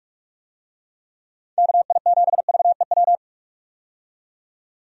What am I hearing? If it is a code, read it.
Morse code, heard as a keyed tone, about 42 wpm, 700 Hz. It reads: KI7VEW
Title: USA Call Signs - 6 Characters - 45wpm